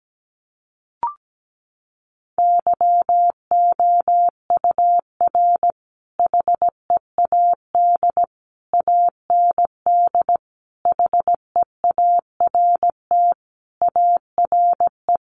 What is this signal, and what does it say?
Morse code, heard as a keyed tone, about 17 wpm, 700 Hz. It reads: YOUR HEAD AND HEART ARE
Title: Sentences from Top 1300 Words 17wpm